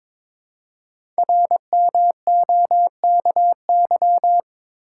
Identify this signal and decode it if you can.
Morse code, heard as a keyed tone, about 22 wpm, 690 Hz. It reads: RMOKY
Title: Code Groups - 5 Letters - 22wpm